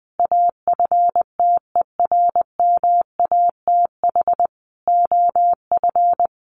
Morse code, heard as a keyed tone, about 19 wpm, 700 Hz. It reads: AFTERMATH OF
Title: Sets of 5 Words 20wpm